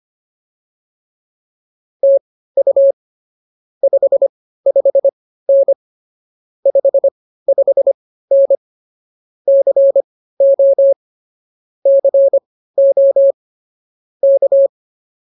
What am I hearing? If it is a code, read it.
Morse code, heard as a keyed tone, about 24 wpm, spaced as for 15 wpm, 550 Hz. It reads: TU 55N 55N CO CO K
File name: POTA103 15